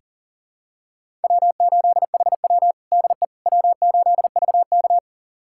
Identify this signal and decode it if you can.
Morse code, heard as a keyed tone, about 37 wpm, 700 Hz. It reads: W8HW DE W8VK
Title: USA Callsign DE Callsign - 4 Characters - 40wpm